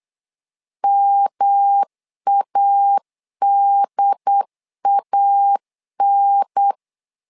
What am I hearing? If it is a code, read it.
Morse code, heard as a keyed tone, about 8 wpm, 800 Hz. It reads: MADAN
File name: LEZIONI / 3 MWRO / 3 M 8wpm 2300